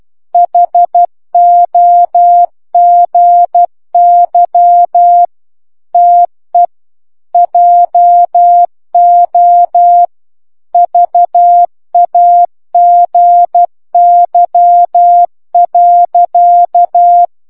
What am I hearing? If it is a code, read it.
Morse code, heard as a keyed tone, about 12 wpm, 700 Hz. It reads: HOGY TE JO VAGY.